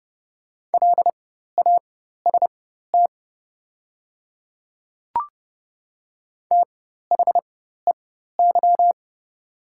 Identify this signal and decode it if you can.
Morse code, heard as a keyed tone, about 29 wpm, spaced as for 14 wpm, 700 Hz. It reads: LAST THEY
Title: 4x Character Spacing - Top 500 Words - 30wpm